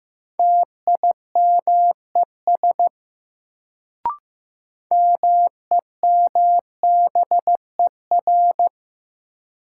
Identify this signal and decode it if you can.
Morse code, heard as a keyed tone, about 15 wpm, 700 Hz. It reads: TIMES MEMBER
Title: Sentences from Top 1100 Words - New Word Drill Alt - 15wpm